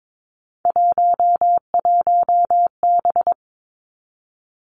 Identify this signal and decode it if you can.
Morse code, heard as a keyed tone, about 22 wpm, 700 Hz. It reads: 11B
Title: Three Challenging Letter-Number Combinations 22wpm